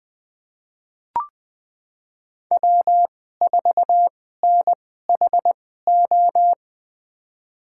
Morse code, heard as a keyed tone, about 20 wpm, spaced as for 14 wpm, 700 Hz. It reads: W4NHO
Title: USA Call Signs - 5 Characters - 2x Character Spacing - 20wpm